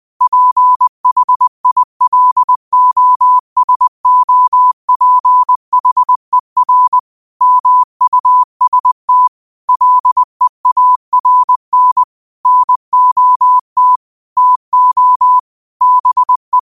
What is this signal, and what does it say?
Morse code, heard as a keyed tone, about 20 wpm, 1000 Hz. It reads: PHILOSOPHER MUST LEARN NOT TO BE
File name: qotdfiles / 20wpm